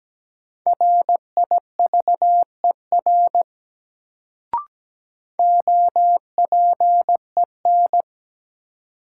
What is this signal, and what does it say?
Morse code, heard as a keyed tone, about 17 wpm, 700 Hz. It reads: RIVER OPEN
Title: Sentences from Top 300 Words - New Word Drill - 17wpm